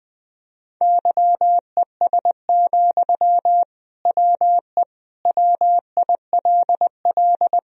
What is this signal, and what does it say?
Morse code, heard as a keyed tone, about 19 wpm, 700 Hz. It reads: YES, WE WILL
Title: Sentences from Top 1000 Words - Review Part 2 - 20wpm